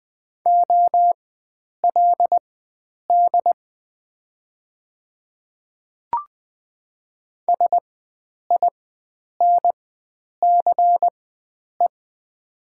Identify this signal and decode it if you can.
Morse code, heard as a keyed tone, about 20 wpm, spaced as for 9 wpm, 700 Hz. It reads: OLD SINCE
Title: 4x Character Spacing - Top 500 Words - 20wpm